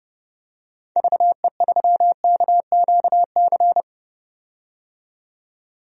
Morse code, heard as a keyed tone, about 32 wpm, 700 Hz. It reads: VE3KQC
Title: DXCC-100 Call Signs - 6 Characters - 30wpm